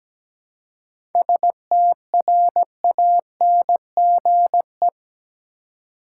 Morse code, heard as a keyed tone, about 17 wpm, 700 Hz. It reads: STRANGE
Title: Seven Letter Words - 17wpm